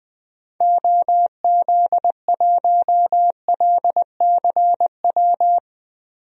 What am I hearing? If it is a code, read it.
Morse code, heard as a keyed tone, about 20 wpm, 700 Hz. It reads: OZ1LCW
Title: DXCC-100 Call Signs - 6 Characters - Repeated 2x - 20wpm